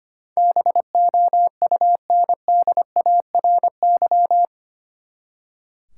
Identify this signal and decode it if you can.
Morse code, heard as a keyed tone, about 25 wpm, 700 Hz. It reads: BOUNDARY
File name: Eight Letter Words - 25wpm